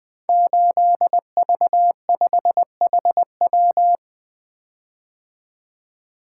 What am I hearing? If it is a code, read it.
Morse code, heard as a keyed tone, about 20 wpm, 700 Hz. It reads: 8V5HW
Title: Code Groups - 5 Letter:Numbers - 20wpm